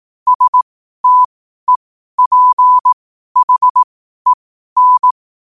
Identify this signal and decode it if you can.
Morse code, heard as a keyed tone, about 18 wpm, spaced as for 12 wpm, 1000 Hz. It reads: STEPHEN
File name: qotdfiles / 12wpm